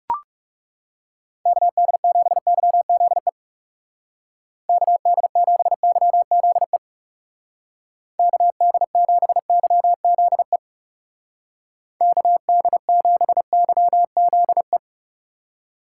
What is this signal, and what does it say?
Morse code, heard as a keyed tone, about 36 wpm, 700 Hz. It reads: KD7YZE KD7YZE KD7YZE KD7YZE
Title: Speed Racing - USA Call Signs 30wpm